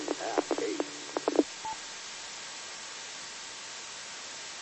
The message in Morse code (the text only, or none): none